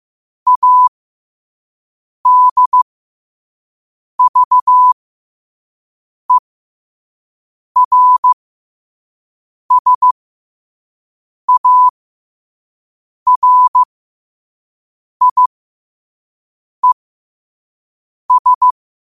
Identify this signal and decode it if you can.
Morse code, heard as a keyed tone, about 15 wpm, spaced as for 5 wpm, 1000 Hz. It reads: ADVERSARIES
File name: qotdfiles / 5wpm